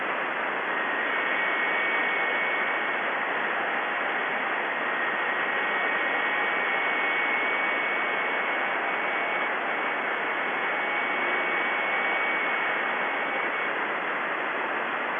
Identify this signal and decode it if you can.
FT8.
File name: contests / 2019-june=vhf-k5tr- -0101 0-2